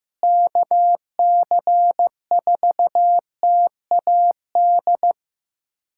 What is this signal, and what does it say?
Morse code, heard as a keyed tone, about 16 wpm, 700 Hz. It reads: KC4TAD
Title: USA Callsign DE Callsign - 6 Characters - 15wpm